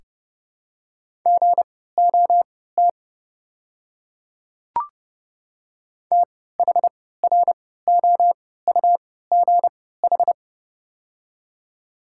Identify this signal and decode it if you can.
Morse code, heard as a keyed tone, about 30 wpm, spaced as for 17 wpm, 690 Hz. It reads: GOT THROUGH
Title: 3x Character Spacing - Top 300 Words - 30wpm